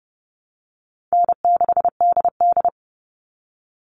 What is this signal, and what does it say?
Morse code, heard as a keyed tone, about 30 wpm, 700 Hz. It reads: N6DD
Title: All Cut Numbers - 4-Digit 30wpm